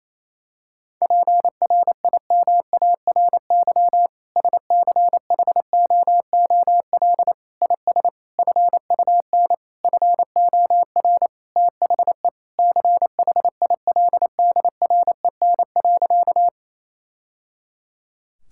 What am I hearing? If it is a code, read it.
Morse code, heard as a keyed tone, about 28 wpm, 700 Hz. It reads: PRIMARY SCHOOL IS FUN FOR THE CHILDREN.
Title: Sentences from Top 1000 Words - Review Part 5 - 28wpm